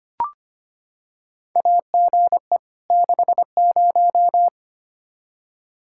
Morse code, heard as a keyed tone, about 25 wpm, 700 Hz. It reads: AGE 60
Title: Practice QSO - Exchange 3 - More Info - 25wpm